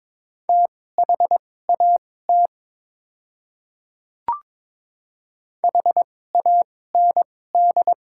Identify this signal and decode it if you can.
Morse code, heard as a keyed tone, about 22 wpm, spaced as for 16 wpm, 690 Hz. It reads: THAT HAND
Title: 2x Character Spacing - Top 200 Words - 22wpm